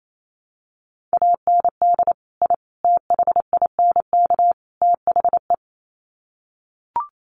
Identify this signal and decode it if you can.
Morse code, heard as a keyed tone, about 28 wpm, 700 Hz. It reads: AND I THINK THE
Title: Sets of 4 Words 28wpm